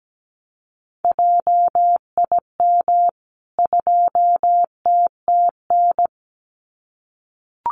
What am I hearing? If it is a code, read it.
Morse code, heard as a keyed tone, about 17 wpm, 700 Hz. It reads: JIM 2TTN
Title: CWT Exchange 17wpm